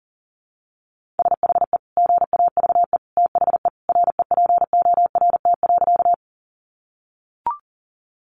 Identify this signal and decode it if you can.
Morse code, heard as a keyed tone, about 40 wpm, 700 Hz. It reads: SHE GAVE THE REPORT.
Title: Sentences from Top 900 Words 40wpm